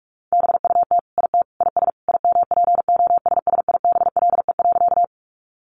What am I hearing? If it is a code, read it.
Morse code, heard as a keyed tone, about 44 wpm, 700 Hz. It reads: BUT IT IS IMPOSSIBLE.
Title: Sentences from Top 1100 Words 45wpm